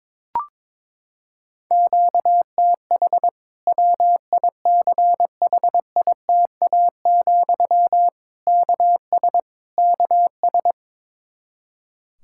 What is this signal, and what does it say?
Morse code, heard as a keyed tone, about 22 wpm, 700 Hz. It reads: QTH WICHITA, KS KS
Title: K1USN SST QSO Practice 22wpm